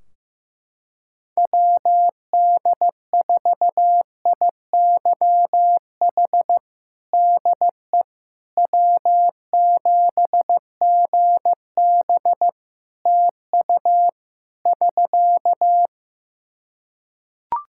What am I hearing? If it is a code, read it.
Morse code, heard as a keyed tone, about 15 wpm, 700 Hz. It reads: WD4IYH DE W7GB TU <SK>
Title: Practice QSO - Exchange 4 - Ending - 15wpm